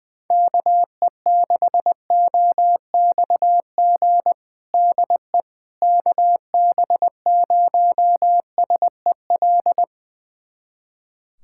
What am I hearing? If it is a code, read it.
Morse code, heard as a keyed tone, about 21 wpm, 700 Hz. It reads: KE6OXG DE KB0SEL
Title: USA Callsign DE Callsign - 6 Characters - 20wpm